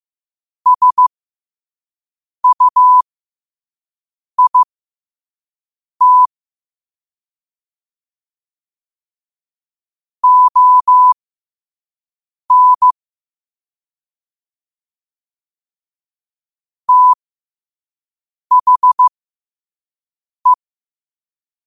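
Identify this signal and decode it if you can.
Morse code, heard as a keyed tone, about 15 wpm, spaced as for 5 wpm, 1000 Hz. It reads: SUIT ON THE
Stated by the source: Morse code quotes sent at 5 words per minute.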